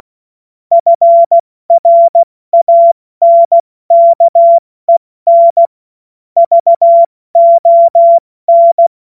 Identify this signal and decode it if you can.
Morse code, heard as a keyed tone, about 16 wpm, spaced as for 14 wpm, 680 Hz. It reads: FRANKEN VON
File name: Morsekurs / Witze / Witz 29. August 2023 70 F80